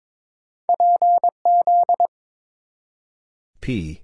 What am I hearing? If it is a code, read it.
Morse code, heard as a keyed tone, about 22 wpm, 700 Hz. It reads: PZ
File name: practice / 22wpm / Two Letter-Number Combinations - 22wpm